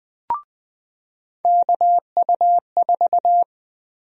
Morse code, heard as a keyed tone, about 20 wpm, 700 Hz. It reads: KU4
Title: USA Call Sign Prefixes - Repeated 2x - 20wpm